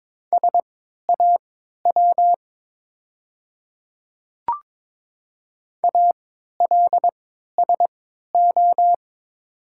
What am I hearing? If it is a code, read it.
Morse code, heard as a keyed tone, about 22 wpm, spaced as for 12 wpm, 700 Hz. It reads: SAW ALSO
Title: 3x Character Spacing - Top 300 Words - 22wpm